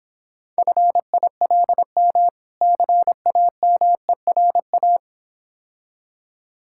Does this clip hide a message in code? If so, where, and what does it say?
Yes: Morse code from 0.6 to 5.0 s, heard as a keyed tone, about 26 wpm, 700 Hz. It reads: FILM CAMERA